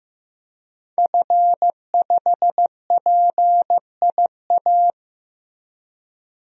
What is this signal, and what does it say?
Morse code, heard as a keyed tone, about 15 wpm, 700 Hz. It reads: F5PIA
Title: DXCC-100 Call Signs - 5 Characters - Repeated 3x - 15wpm